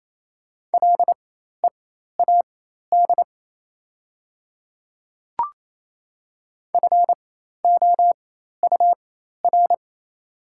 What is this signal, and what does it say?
Morse code, heard as a keyed tone, about 28 wpm, spaced as for 13 wpm, 700 Hz. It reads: LEAD FOUR
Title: 4x Character Spacing - Top 500 Words - 28wpm